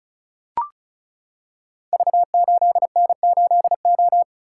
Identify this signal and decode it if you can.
Morse code, heard as a keyed tone, about 36 wpm, 690 Hz. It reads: V8N8O
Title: Code Groups - 5 Letter:Numbers - 35wpm